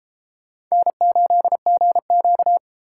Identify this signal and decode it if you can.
Morse code, heard as a keyed tone, about 33 wpm, 700 Hz. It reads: N8GQ